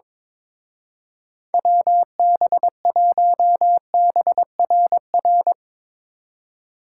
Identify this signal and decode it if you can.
Morse code, heard as a keyed tone, about 24 wpm, 700 Hz. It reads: WB1BRR
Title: Speed Racing - USA 2x3 Call Signs 22wpm